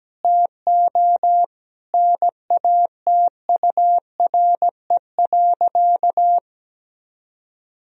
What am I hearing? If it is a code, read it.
Morse code, heard as a keyed tone, about 17 wpm, 700 Hz. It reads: TO NATURE.
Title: Sentences from Top 1000 Words - Review Part 4 - 17wpm